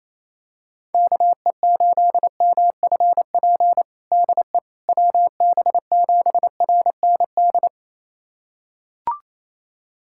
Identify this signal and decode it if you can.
Morse code, heard as a keyed tone, about 28 wpm, 700 Hz. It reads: KE8MFP DE WB7RND
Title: USA Callsign DE Callsign - 6 Characters - 28wpm